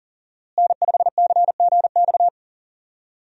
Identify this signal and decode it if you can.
Morse code, heard as a keyed tone, about 37 wpm, 700 Hz. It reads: N5CGX
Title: USA Call Signs - 5 Characters - Repeated 3x - 40wpm